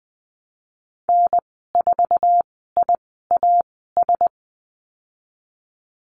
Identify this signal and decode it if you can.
Morse code, heard as a keyed tone, about 20 wpm, spaced as for 14 wpm, 700 Hz. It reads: N4IAS
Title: 055 Lesson 10g - Letter r - Call Signs 20wpm